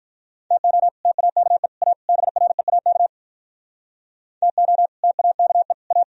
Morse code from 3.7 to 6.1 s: TO TAKE A

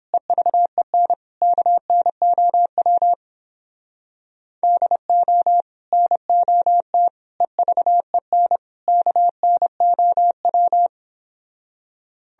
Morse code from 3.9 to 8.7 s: DO NOT EVEN

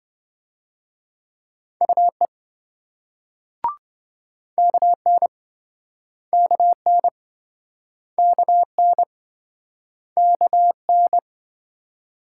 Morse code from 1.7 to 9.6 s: UE KN KN KN

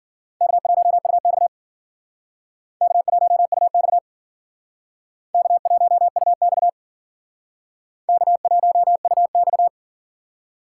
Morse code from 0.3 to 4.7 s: K1UX K1UX

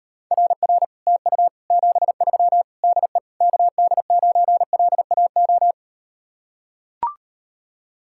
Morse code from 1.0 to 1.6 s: TU

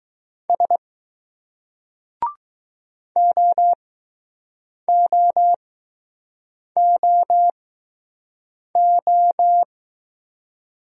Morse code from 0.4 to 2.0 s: S

